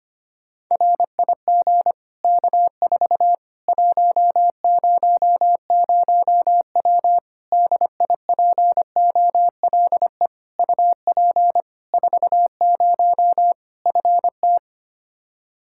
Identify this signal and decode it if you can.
Morse code, heard as a keyed tone, about 25 wpm, 700 Hz. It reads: RIG K4 100W DIPOLE UP 40 FT
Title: Practice QSO - Exchange 2 - Info - 25wpm